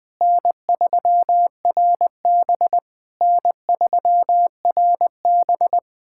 Morse code, heard as a keyed tone, about 20 wpm, 700 Hz. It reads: N3RB N3RB
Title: SOTA QSO Practice 20wpm